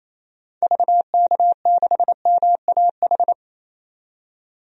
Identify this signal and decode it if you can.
Morse code, heard as a keyed tone, about 28 wpm, 700 Hz. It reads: VK6MAH
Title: DXCC-100 Call Signs - 6 Characters - 28wpm